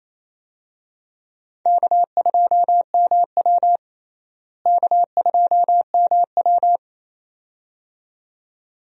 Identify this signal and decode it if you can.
Morse code, heard as a keyed tone, about 28 wpm, 700 Hz. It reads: K2MW K2MW
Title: USA Call Signs - 4 Characters - Repeated 2x - 28wpm